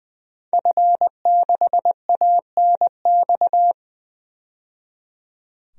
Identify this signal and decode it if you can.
Morse code, heard as a keyed tone, about 20 wpm, 700 Hz. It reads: F6ANX
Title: DXCC-100 Call Signs - 5 Characters - 20wpm